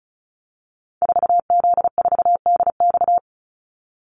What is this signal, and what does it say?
Morse code, heard as a keyed tone, about 34 wpm, 700 Hz. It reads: 4Z4DX
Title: Call Signs Repeated 2x - 35wpm